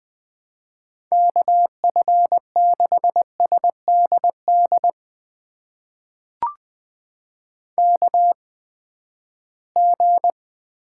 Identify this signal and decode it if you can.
Morse code, heard as a keyed tone, about 20 wpm, spaced as for 14 wpm, 700 Hz. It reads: KF6SDD KG
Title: USA Call Signs - 6 Characters - 8x Character Spacing - 20wpm